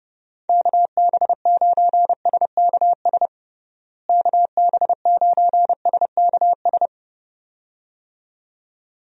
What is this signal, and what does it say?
Morse code, heard as a keyed tone, about 30 wpm, 700 Hz. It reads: KB9SKS KB9SKS
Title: USA Call Signs - 6 Characters - Repeated 2x - 30wpm